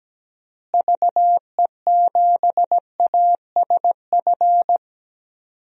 Morse code, heard as a keyed tone, about 17 wpm, 700 Hz. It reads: VE7ASF
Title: DXCC-100 Call Signs - 6 Characters - Repeated 3x - 17wpm